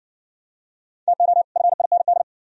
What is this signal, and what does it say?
Morse code, heard as a keyed tone, about 57 wpm, 700 Hz. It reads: TO FIND